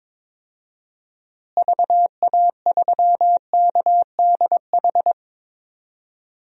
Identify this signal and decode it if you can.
Morse code, heard as a keyed tone, about 22 wpm, 700 Hz. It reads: VA3KDH
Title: DXCC-100 Call Signs - 6 Characters - Repeated 2x - 22wpm